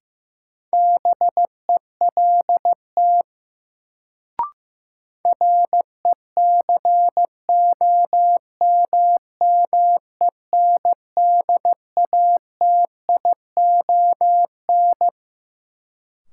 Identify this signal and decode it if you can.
Morse code, heard as a keyed tone, about 16 wpm, 700 Hz. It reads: BELT RECOMMENDATION
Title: Sentences from Top 1400 Words - New Word Drill Alt - 15wpm